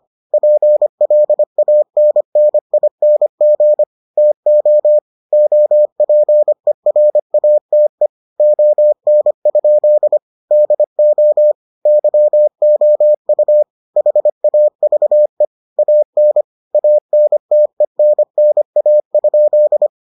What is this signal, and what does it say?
Morse code, heard as a keyed tone, about 25 wpm, 600 Hz. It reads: PLANNING TO OPERATE ON? DO YOU HAVE AN ANTENNA?